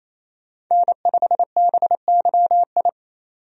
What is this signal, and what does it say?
Morse code, heard as a keyed tone, about 28 wpm, 700 Hz. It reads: N5BYI
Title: USA Call Signs - 5 Characters - 28wpm